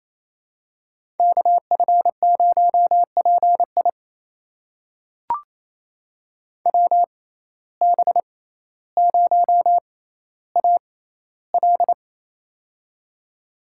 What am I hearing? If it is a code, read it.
Morse code, heard as a keyed tone, about 28 wpm, spaced as for 14 wpm, 700 Hz. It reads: KF0PI WB0AL
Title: USA Call Signs - 5 Characters - 6x Character Spacing - 28wpm